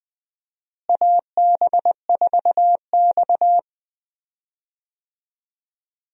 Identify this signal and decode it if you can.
Morse code, heard as a keyed tone, about 20 wpm, 700 Hz. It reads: AB4X
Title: USA Call Signs - 4 Characters - Repeated 3x - 20wpm